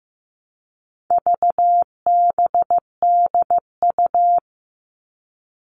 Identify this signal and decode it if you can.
Morse code, heard as a keyed tone, about 15 wpm, 700 Hz. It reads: VBDU
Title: All Cut Numbers - 4-Digit 15wpm